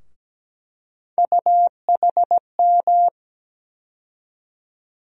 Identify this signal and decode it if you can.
Morse code, heard as a keyed tone, about 17 wpm, 700 Hz. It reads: UHM
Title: USA Call Sign Postfixes - 3 Letters 17wpm